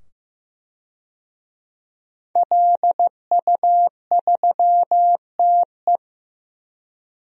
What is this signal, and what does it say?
Morse code, heard as a keyed tone, about 15 wpm, 700 Hz. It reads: LU3TE
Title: DXCC-100 Call Signs - 5 Characters - 15wpm